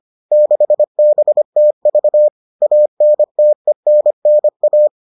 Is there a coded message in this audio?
Yes: Morse code from 0.3 to 4.9 s, heard as a keyed tone, about 25 wpm, 600 Hz. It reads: 6BTV ANTENNA